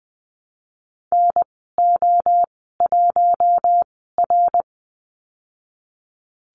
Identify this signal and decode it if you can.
Morse code, heard as a keyed tone, about 20 wpm, spaced as for 15 wpm, 700 Hz. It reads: NO1R
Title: 055 Lesson 10g - Letter r - Call Signs 20wpm